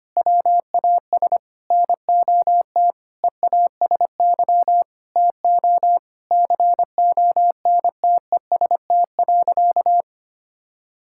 Morse code, heard as a keyed tone, about 25 wpm, 700 Hz. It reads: WAS NOT EASY TO CONTEST.
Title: Sentences from Top 1600 Words 25wpm